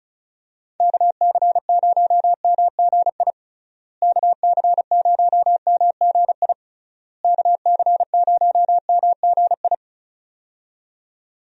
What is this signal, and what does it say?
Morse code, heard as a keyed tone, about 36 wpm, 700 Hz. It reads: KC0MGI KC0MGI KC0MGI
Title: USA Call Signs - 6 Characters - Repeated 3x - 35wpm